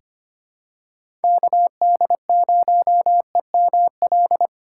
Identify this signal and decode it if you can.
Morse code, heard as a keyed tone, about 25 wpm, 700 Hz. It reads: KD0EML
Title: USA Call Signs - 6 Characters - 2x Character Spacing - 25wpm